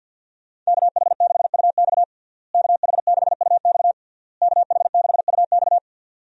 Morse code, heard as a keyed tone, about 52 wpm, 700 Hz. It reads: KH6UX KH6UX KH6UX
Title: USA Call Signs - 5 Characters - Repeated 3x - 50wpm